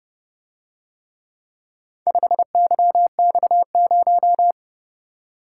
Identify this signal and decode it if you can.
Morse code, heard as a keyed tone, about 30 wpm, 700 Hz. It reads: 5YX0